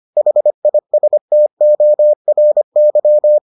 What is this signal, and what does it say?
Morse code, heard as a keyed tone, about 25 wpm, 600 Hz. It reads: HISTORY